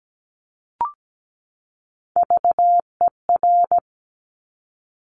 Morse code, heard as a keyed tone, about 17 wpm, 700 Hz. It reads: VER
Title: Top Three Letter Combinations in Words 17wpm